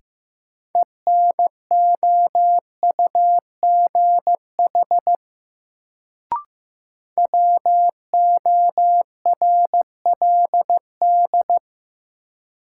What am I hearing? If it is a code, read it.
Morse code, heard as a keyed tone, about 15 wpm, 700 Hz. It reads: ENOUGH WORLD
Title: Sets of 2 Words - New Word Drill - 15wpm